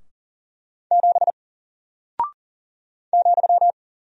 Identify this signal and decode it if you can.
Morse code, heard as a keyed tone, about 40 wpm, 700 Hz. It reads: 7 ,
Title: Single Character (Letter:Number:Punctuation:Prosign) Alt 40wpm